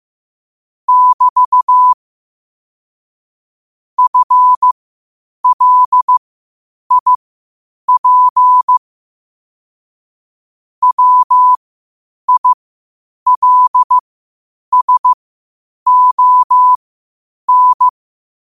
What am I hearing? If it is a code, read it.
Morse code, heard as a keyed tone, about 15 wpm, spaced as for 8 wpm, 1000 Hz. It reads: <BT> FLIP WILSON